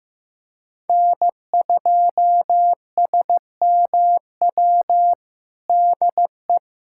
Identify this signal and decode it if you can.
Morse code, heard as a keyed tone, about 15 wpm, 700 Hz. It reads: N2SMW DE
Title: USA Callsign DE Callsign - 5 Characters - 15wpm